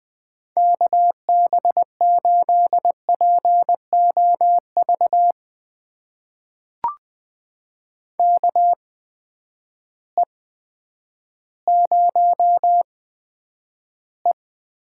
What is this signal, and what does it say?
Morse code, heard as a keyed tone, about 20 wpm, spaced as for 10 wpm, 700 Hz. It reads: KB8POV KE0E
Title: USA Call Signs - 6 Characters - 8x Character Spacing - 20wpm